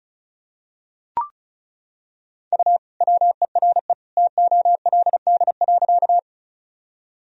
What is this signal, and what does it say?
Morse code, heard as a keyed tone, about 34 wpm, 700 Hz. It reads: U WERE TOLD.
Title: Sentences from Top 400 Words and Abbreviations 35wpm